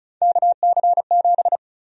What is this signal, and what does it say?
Morse code, heard as a keyed tone, about 36 wpm, 700 Hz. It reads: KC7
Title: USA Call Sign Prefixes - Repeated 2x - 35wpm